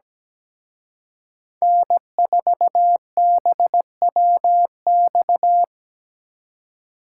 Morse code, heard as a keyed tone, about 20 wpm, 700 Hz. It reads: N4BWX
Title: Speed Racing - USA Call Signs 17wpm